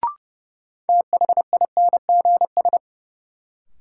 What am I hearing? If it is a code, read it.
Morse code, heard as a keyed tone, about 32 wpm, 700 Hz. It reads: THINGS
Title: Sentences from Top 1000 Words - Review Part 4 - New Word Drill - 30wpm